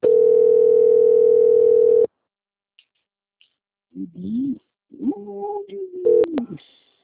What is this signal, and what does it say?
Ringback tone (North America), 440 + 480 Hz.
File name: out- - - - - .8284